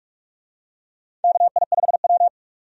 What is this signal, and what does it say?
Morse code, heard as a keyed tone, about 46 wpm, 700 Hz. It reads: KI5W
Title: USA Call Signs - 4 Characters - 45wpm